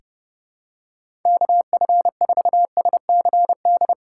Morse code, heard as a keyed tone, about 30 wpm, 700 Hz. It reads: KF4SCD
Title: USA Call Signs - 6 Characters - 4x Character Spacing - 30wpm